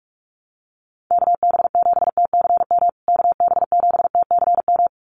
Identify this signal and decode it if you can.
Morse code, heard as a keyed tone, about 44 wpm, 700 Hz. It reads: KB7TCM KB7TCM
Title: USA Call Signs Repeated 2x 45wpm